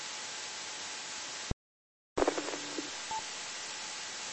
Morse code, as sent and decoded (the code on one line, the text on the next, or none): none